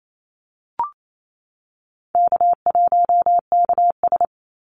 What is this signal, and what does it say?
Morse code, heard as a keyed tone, about 28 wpm, 700 Hz. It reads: K1KS
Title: Easy Call Signs - 28wpm